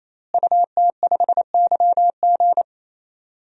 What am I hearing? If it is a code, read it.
Morse code, heard as a keyed tone, about 28 wpm, 700 Hz. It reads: UT5YG
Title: DXCC-100 Call Signs - 5 Characters - Repeated 3x - 28wpm